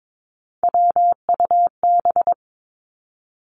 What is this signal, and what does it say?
Morse code, heard as a keyed tone, about 22 wpm, 700 Hz. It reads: WUB
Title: Three Challenging Letter-Number Combinations 22wpm